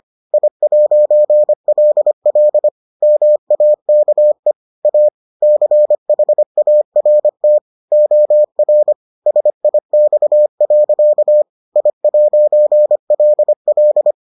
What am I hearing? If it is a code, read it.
Morse code, heard as a keyed tone, about 25 wpm, 600 Hz. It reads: I'LL MAKE A CHART OR SIX. I'LL